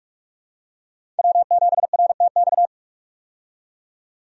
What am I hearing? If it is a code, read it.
Morse code, heard as a keyed tone, about 43 wpm, 700 Hz. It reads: W7RTX
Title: USA Call Signs - 5 Characters - 45wpm